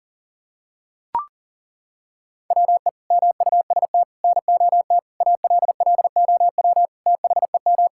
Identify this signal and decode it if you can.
Morse code, heard as a keyed tone, about 40 wpm, 700 Hz. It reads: WE MUST NOT ALLOW THEM
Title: Sentences from Top 1000 Words - Review Part 5 - 40wpm